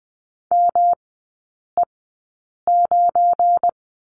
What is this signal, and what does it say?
Morse code, heard as a keyed tone, about 20 wpm, spaced as for 8 wpm, 700 Hz. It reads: ME9
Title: 172 Lesson 28e - Number 9 - Three Characters 20wpm